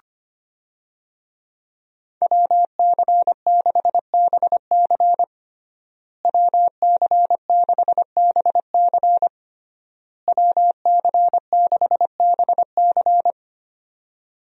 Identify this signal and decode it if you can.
Morse code, heard as a keyed tone, about 25 wpm, 700 Hz. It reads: WC6BC WC6BC WC6BC
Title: USA Call Signs - 5 Characters - Repeated 3x - 25wpm